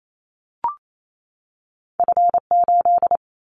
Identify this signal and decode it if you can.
Morse code, heard as a keyed tone, about 28 wpm, 700 Hz. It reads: F8
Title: Callsign Prefixes for Top 50 DXCC - 28wpm